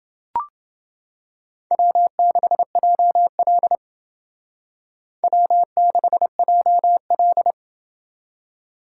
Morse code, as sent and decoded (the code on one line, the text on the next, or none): .-- -.... .--- .-.. / .-- -.... .--- .-..
W6JL W6JL